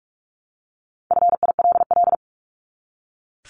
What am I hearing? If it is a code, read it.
Morse code, heard as a keyed tone, about 44 wpm, 700 Hz. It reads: FILL
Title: ED Suffix - Top 100 Words with Base Words 45wpm